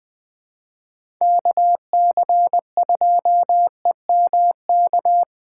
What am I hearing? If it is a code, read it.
Morse code, heard as a keyed tone, about 20 wpm, 700 Hz. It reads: KC2EMK
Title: USA Call Signs - 6 Characters - 4x Character Spacing - 20wpm